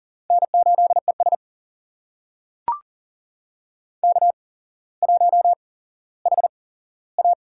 Morse code, heard as a keyed tone, about 40 wpm, spaced as for 17 wpm, 700 Hz. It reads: N8ES K1HA
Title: USA Call Signs - 4 Characters - 8x Character Spacing - 40wpm